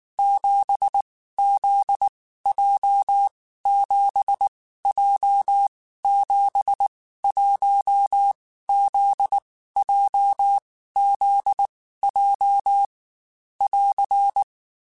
Morse code, heard as a keyed tone, about 19 wpm, 800 Hz. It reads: 7 Z J 7 J 7 1 Z J Z J <AR>
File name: Basic Sets / Basic Letter Groups Collective7 / JZ17@18